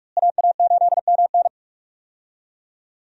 Morse code, heard as a keyed tone, about 46 wpm, 700 Hz. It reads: AA8MN
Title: USA Call Signs - 5 Characters - Repeated 3x - 45wpm